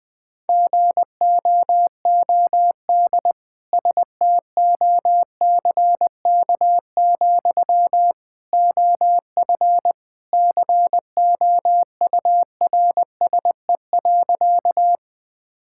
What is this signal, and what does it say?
Morse code, heard as a keyed tone, about 20 wpm, 700 Hz. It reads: GOOD STOCK, OF COURSE.
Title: Sentences from Top 1000 Words - Review Part 5 - 20wpm